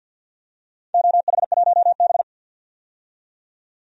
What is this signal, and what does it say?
Morse code, heard as a keyed tone, about 46 wpm, 700 Hz. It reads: OH1B
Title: DXCC-100 Call Signs - 4 Characters - 50wpm